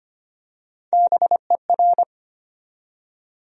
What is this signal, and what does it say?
Morse code, heard as a keyed tone, about 25 wpm, 700 Hz. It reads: BER
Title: USA Call Sign Postfixes - 3 Letters 25wpm